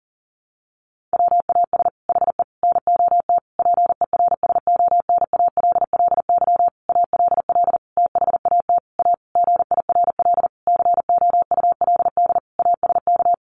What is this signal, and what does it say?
Morse code, heard as a keyed tone, about 40 wpm, 700 Hz. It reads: WAS HE NOT PERSONALLY ALL THAT A GIRL COULD ASK
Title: Sentences from Top 1400 Words 40wpm